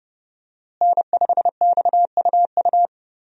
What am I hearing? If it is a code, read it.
Morse code, heard as a keyed tone, about 30 wpm, 700 Hz. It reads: N5XUU
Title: USA Call Signs - 5 Characters - Repeated 2x - 30wpm